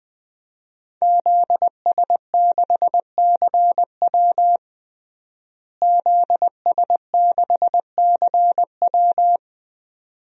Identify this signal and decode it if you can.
Morse code, heard as a keyed tone, about 20 wpm, 700 Hz. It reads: ZS6CW ZS6CW
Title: DXCC-100 Call Signs - 5 Characters - Repeated 2x - 20wpm